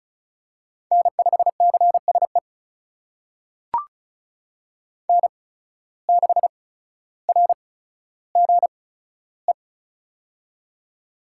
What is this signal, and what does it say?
Morse code, heard as a keyed tone, about 36 wpm, spaced as for 15 wpm, 700 Hz. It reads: N5CSE N6RGE
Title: USA Call Signs - 5 Characters - 8x Character Spacing - 35wpm